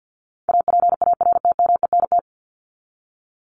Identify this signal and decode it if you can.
Morse code, heard as a keyed tone, about 50 wpm, 700 Hz. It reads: APARTMENT
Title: Top 751-1000 Words 50wpm